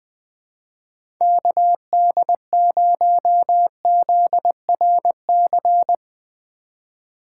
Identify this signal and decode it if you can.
Morse code, heard as a keyed tone, about 20 wpm, 700 Hz. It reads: KD0ZRC
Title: USA Call Signs - 3x Character Spacing - 20wpm